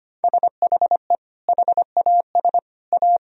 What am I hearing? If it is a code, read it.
Morse code, heard as a keyed tone, about 25 wpm, 700 Hz. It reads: SHE HAS A
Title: Sentences from Top 1000 Words - Review Part 2 - 25wpm